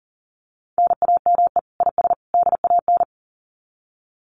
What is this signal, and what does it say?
Morse code, heard as a keyed tone, about 40 wpm, 700 Hz. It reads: NAME IS DAN
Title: Common QSO Phrases 40wpm